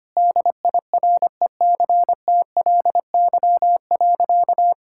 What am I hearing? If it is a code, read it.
Morse code, heard as a keyed tone, about 25 wpm, 700 Hz. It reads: DIRECTLY.
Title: Sentences from Top 1000 Words - Review Part 5 - 25wpm